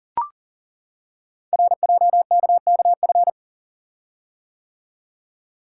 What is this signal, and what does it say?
Morse code, heard as a keyed tone, about 39 wpm, 700 Hz. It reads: RJKKF
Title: Code Groups - 5 Letters - 40wpm